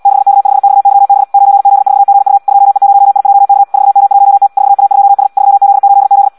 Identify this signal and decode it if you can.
Morse code, heard as a keyed tone, about 76 wpm, 800 Hz. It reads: 6K5XYB 9Z4GU 2E0IJK 4M1T 4A2S 3Z8Z
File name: Audio practica CW / qrq-tx-78 2